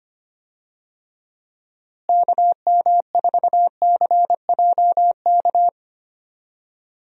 Morse code, heard as a keyed tone, about 25 wpm, 700 Hz. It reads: KM4CJK
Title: USA Call Signs - 6 Characters - Repeated 3x - 25wpm